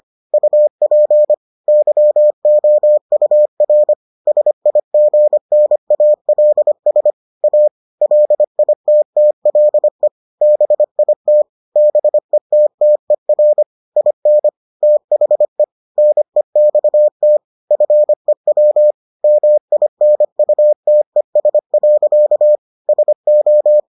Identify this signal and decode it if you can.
Morse code, heard as a keyed tone, about 25 wpm, 600 Hz. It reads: UP YOUR SIGNALS A LITTLE BIT BETTER IN THE NEXT FEW MINUTES. SO